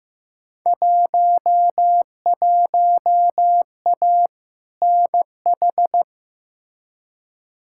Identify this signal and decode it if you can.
Morse code, heard as a keyed tone, about 16 wpm, 700 Hz. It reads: 11A NH
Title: Field Day Exchanges Encore - 15wpm